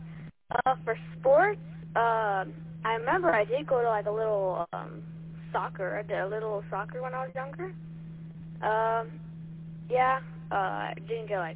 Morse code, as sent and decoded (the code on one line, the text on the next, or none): none